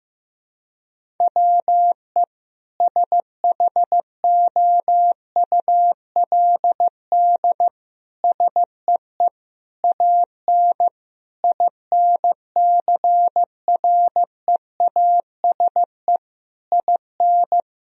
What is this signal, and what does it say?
Morse code, heard as a keyed tone, about 15 wpm, 700 Hz. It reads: WE SHOULD SEE AN INCREASE IN
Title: Sentences from Top 1000 Words - Review Part 4 - 15wpm